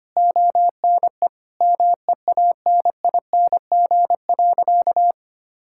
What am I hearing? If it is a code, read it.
Morse code, heard as a keyed tone, about 25 wpm, 700 Hz. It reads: ONE MEANING.
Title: Sentences from Top 1000 Words - Review Part 6 - 25wpm